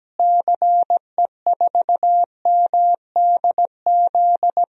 Morse code, heard as a keyed tone, about 17 wpm, 700 Hz. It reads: CE4MDZ
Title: DXCC-100 Call Signs - 6 Characters - Repeated 3x - 17wpm